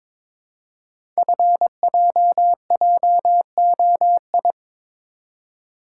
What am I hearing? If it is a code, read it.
Morse code, heard as a keyed tone, about 22 wpm, 700 Hz. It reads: FJJOI
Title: Code Groups - 5 Letters - 22wpm